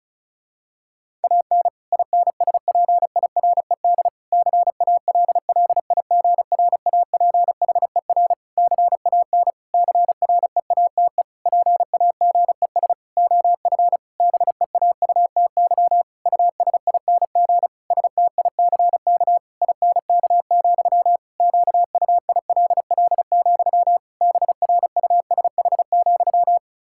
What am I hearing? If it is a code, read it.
Morse code, heard as a keyed tone, about 34 wpm, 700 Hz. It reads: AN INSPIRED CALLIGRAPHER CAN CREATE PAGES OF BEAUTY USING STICK INK, QUILL, BRUSH,
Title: Pangram Challenge v2 - 35wpm